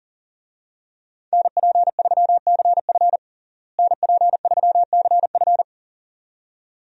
Morse code, heard as a keyed tone, about 40 wpm, 700 Hz. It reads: NP3CF NP3CF
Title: USA Call Signs - 5 Characters - Repeated 2x - 40wpm